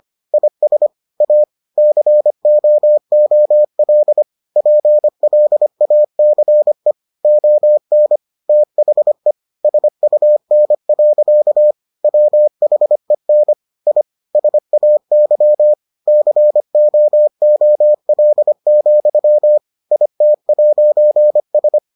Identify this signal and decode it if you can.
Morse code, heard as a keyed tone, about 25 wpm, 600 Hz. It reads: IS A COOL PLACE ON THE SUN. WHEN I SAY COOL, IT'S